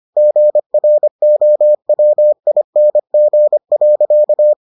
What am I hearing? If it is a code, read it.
Morse code, heard as a keyed tone, about 25 wpm, 600 Hz. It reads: GROWING.